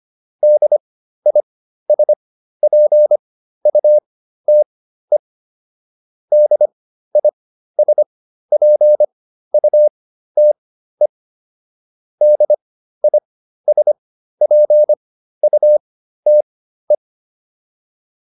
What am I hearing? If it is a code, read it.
Morse code, heard as a keyed tone, about 25 wpm, spaced as for 13 wpm, 600 Hz. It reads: DISPUTE DISPUTE DISPUTE
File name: DIS7 13